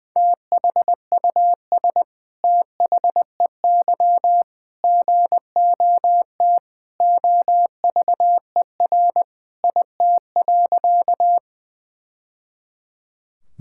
Morse code, heard as a keyed tone, about 21 wpm, 700 Hz. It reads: THUS THEY GOT OVER IT.
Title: Sentences from Top 1000 Words - Review Part 5 - 20wpm